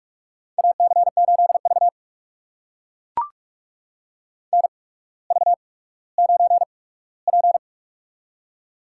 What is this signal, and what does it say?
Morse code, heard as a keyed tone, about 44 wpm, spaced as for 19 wpm, 690 Hz. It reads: AC8V NV9P
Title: USA Call Signs - 4 Characters - 8x Character Spacing - 45wpm